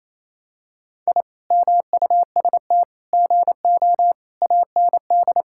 Morse code, heard as a keyed tone, about 28 wpm, 700 Hz. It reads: I MUST GO AND
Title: Sentences from Top 1000 Words - Review Part 5 - 28wpm